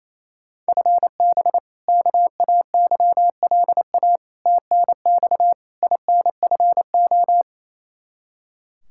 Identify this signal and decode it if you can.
Morse code, heard as a keyed tone, about 28 wpm, 700 Hz. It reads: FB KAYLA TNX INFO
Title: K1USN SST QSO Practice 28wpm